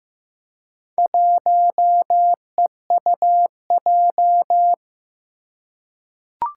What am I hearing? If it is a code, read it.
Morse code, heard as a keyed tone, about 15 wpm, 700 Hz. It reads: 1EUJ
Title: Four Challenging Letter-Number Combinations 15wpm